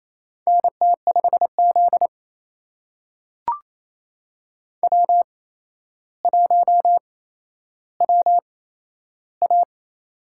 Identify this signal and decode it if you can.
Morse code, heard as a keyed tone, about 28 wpm, spaced as for 12 wpm, 700 Hz. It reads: NT5Z W1WA
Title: USA Call Signs - 4 Characters - 8x Character Spacing - 28wpm